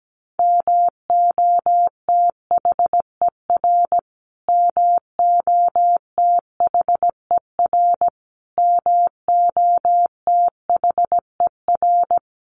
Morse code, heard as a keyed tone, about 17 wpm, 700 Hz. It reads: MOTHER MOTHER MOTHER
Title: Top 300 Words Repeated 17wpm